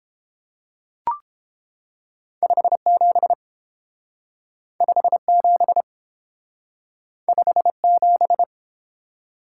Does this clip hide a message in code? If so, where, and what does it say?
Yes: Morse code from 2.4 to 8.4 s, heard as a keyed tone, about 29 wpm, 700 Hz. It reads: 57 57 57